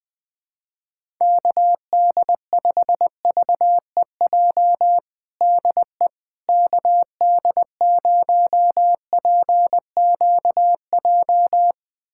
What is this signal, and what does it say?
Morse code, heard as a keyed tone, about 20 wpm, 700 Hz. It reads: KD5VEJ DE KD0PQJ
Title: USA Callsign DE Callsign - 6 Characters - 20wpm